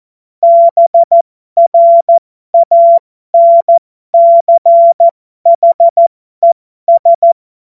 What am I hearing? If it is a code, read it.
Morse code, heard as a keyed tone, about 14 wpm, spaced as for 12 wpm, 680 Hz. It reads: BRANCHES
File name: Morsekurs / Witze